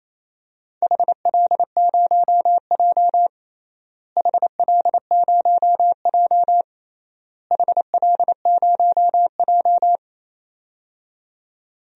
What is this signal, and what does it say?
Morse code, heard as a keyed tone, about 28 wpm, 700 Hz. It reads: HL0J HL0J HL0J
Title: DXCC-100 Call Signs - 4 Characters - Repeated 3x - 28wpm